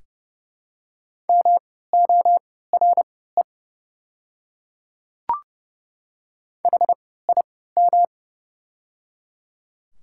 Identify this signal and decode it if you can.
Morse code, heard as a keyed tone, about 30 wpm, spaced as for 17 wpm, 700 Hz. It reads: MORE HIM
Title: 3x Character Spacing - Top 100 Words - 30wpm